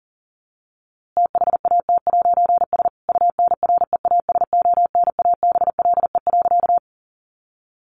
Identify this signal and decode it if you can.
Morse code, heard as a keyed tone, about 40 wpm, 700 Hz. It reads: THAT'S UNREASONABLE.
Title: A Sound Of Thunder 40wpm